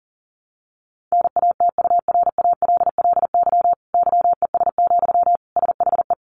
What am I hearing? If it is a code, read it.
Morse code, heard as a keyed tone, about 40 wpm, 700 Hz. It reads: NATURALLY YES, SHE
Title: Sentences from Top 1300 Words 40wpm